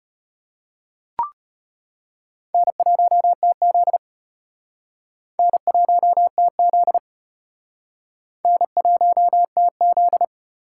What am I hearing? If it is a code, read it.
Morse code, heard as a keyed tone, about 35 wpm, 700 Hz. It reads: N1TZ N1TZ N1TZ